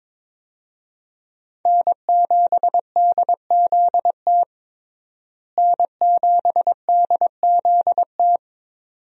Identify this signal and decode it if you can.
Morse code, heard as a keyed tone, about 22 wpm, 700 Hz. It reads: N7DZT N7DZT
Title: USA Call Signs - 5 Characters - Repeated 2x - 22wpm